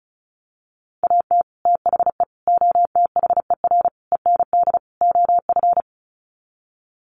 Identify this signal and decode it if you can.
Morse code, heard as a keyed tone, about 34 wpm, 700 Hz. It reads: AT THE OTHER END OF
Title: Sets of 5 Words 35wpm